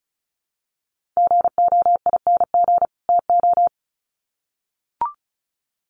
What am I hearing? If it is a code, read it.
Morse code, heard as a keyed tone, about 34 wpm, 700 Hz. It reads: GOING TO
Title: Sets of 2 Words 35wpm